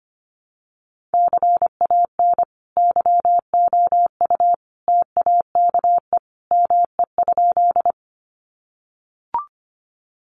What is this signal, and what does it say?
Morse code, heard as a keyed tone, about 25 wpm, 700 Hz. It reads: CAN YOU TAKE ME?
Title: Sentences from Top 200 Words 25wpm